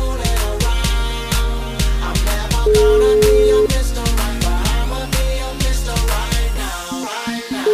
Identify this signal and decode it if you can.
Ringback tone (Europe, 425 Hz), with noise.